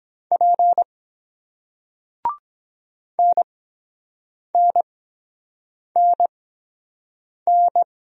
Morse code, heard as a keyed tone, about 23 wpm, spaced as for 7 wpm, 700 Hz. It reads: P NNNN